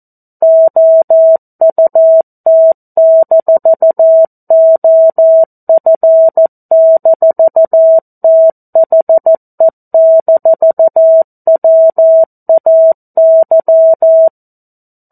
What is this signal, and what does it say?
Morse code, heard as a keyed tone, about 14 wpm, 650 Hz. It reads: OUT-OF-THE-WAY
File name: 02-Chapter-02 14 WPM